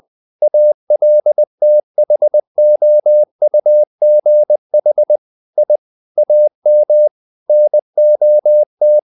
Morse code, heard as a keyed tone, about 20 wpm, 600 Hz. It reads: ALTHOUGH I AM NOT
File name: the old man and the sea / the old man and the sea part 1 20wpm